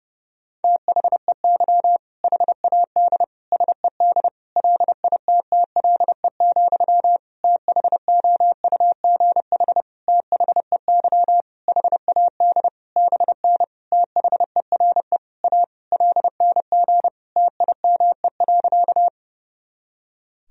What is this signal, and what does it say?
Morse code, heard as a keyed tone, about 32 wpm, 700 Hz. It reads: THEY HAD SED LITTLE, THOUGH THEY HAD BN THERE A LNG TIME.
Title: Sentences from Top 400 Words and Abbreviations 30wpm